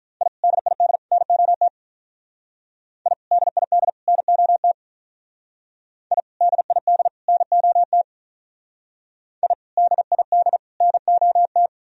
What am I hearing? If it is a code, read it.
Morse code, heard as a keyed tone, about 45 wpm, 700 Hz. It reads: I DID NOT I DID NOT I DID NOT I DID NOT